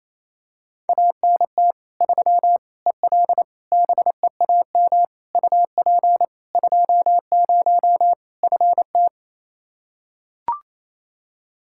Morse code, heard as a keyed tone, about 27 wpm, 700 Hz. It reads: ANT 3 EL BEAM UP 20 FT
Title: Practice QSO - Exchange 2 - Info - 28wpm